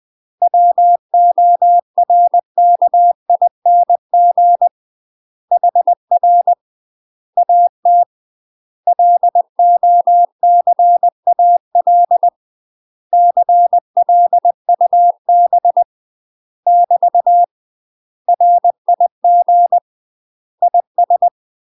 Morse code, heard as a keyed tone, about 20 wpm, 700 Hz. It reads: WORKING HR AT LOCAL CLUB <BT> RIG IS
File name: UK MASTER QSOs TRAINING / 20WPM / UK MASTER QSOs 0068